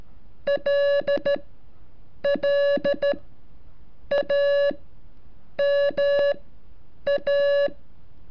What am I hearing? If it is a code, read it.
Morse code, heard as a keyed tone, about 13 wpm, spaced as for 7 wpm, 570 Hz. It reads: LLAMA
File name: curso / ejercicio 34